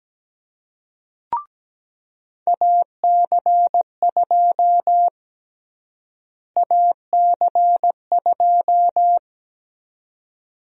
Morse code, heard as a keyed tone, about 17 wpm, 700 Hz. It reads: AC2 AC2
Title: USA Call Sign Prefixes - Repeated 2x - 17wpm